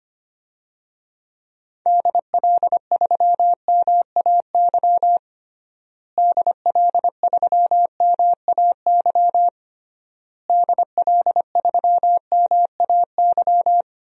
Morse code, heard as a keyed tone, about 25 wpm, 700 Hz. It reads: DL3MAY DL3MAY DL3MAY
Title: DXCC-100 Call Signs - 6 Characters - Repeated 3x - 25wpm